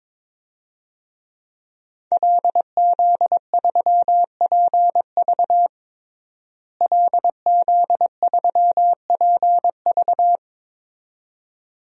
Morse code, heard as a keyed tone, about 22 wpm, 700 Hz. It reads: LZ3PV LZ3PV
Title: DXCC-100 Call Signs - 5 Characters - Repeated 2x - 22wpm